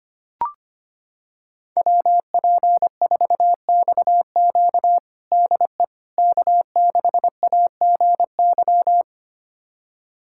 Morse code, heard as a keyed tone, about 25 wpm, 700 Hz. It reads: WP4XQ DE K6AGY
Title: USA Callsign DE Callsign 25wpm